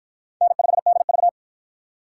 Morse code, heard as a keyed tone, about 47 wpm, 700 Hz. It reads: N5DV